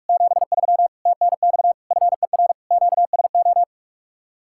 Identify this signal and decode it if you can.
Morse code, heard as a keyed tone, about 49 wpm, 700 Hz. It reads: 73 TNX FER QSO